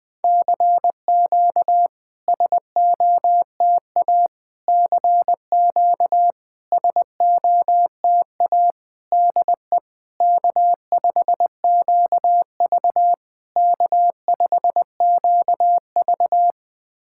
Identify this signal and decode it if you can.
Morse code, heard as a keyed tone, about 20 wpm, 700 Hz. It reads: CQ SOTA CQ SOTA DE K5QV K5QV
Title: SOTA QSO Practice 20wpm